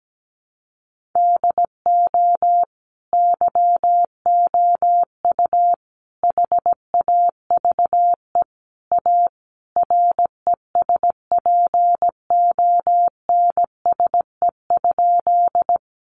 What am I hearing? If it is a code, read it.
Morse code, heard as a keyed tone, about 17 wpm, 700 Hz. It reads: DO YOU HAVE A RESPONSE?
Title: Sentences from Top 1400 Words 17wpm